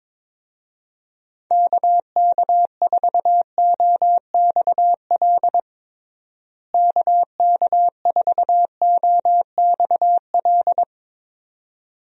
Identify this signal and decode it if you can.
Morse code, heard as a keyed tone, about 22 wpm, 700 Hz. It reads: KK4OXL KK4OXL
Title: USA Call Signs - 6 Characters - Repeated 2x - 22wpm